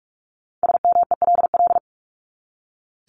Morse code, heard as a keyed tone, about 44 wpm, 700 Hz. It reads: SMELL
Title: Top 1501-1750 Words 45wpm